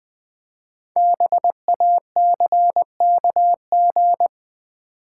Morse code, heard as a keyed tone, about 20 wpm, 700 Hz. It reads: BACKG
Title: Build A Word - 5 Letters At A Time - 10-Letter Words - 20wpm